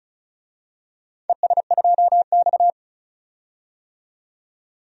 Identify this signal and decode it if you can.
Morse code, heard as a keyed tone, about 35 wpm, 700 Hz. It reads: ES2X
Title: DXCC-100 Call Signs - 4 Characters - 35wpm